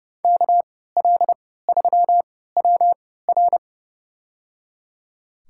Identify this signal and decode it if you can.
Morse code, heard as a keyed tone, about 30 wpm, spaced as for 17 wpm, 700 Hz. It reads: KL3WR
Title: USA Call Signs - 5 Characters - 3x Character Spacing - 30wpm